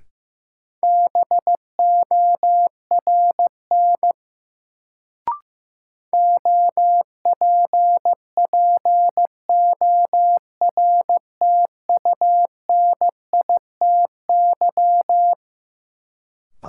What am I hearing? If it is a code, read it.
Morse code, heard as a keyed tone, about 15 wpm, 700 Hz. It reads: BORN OPPORTUNITY
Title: Sentences from Top 1000 Words - New Word Drill - 15wpm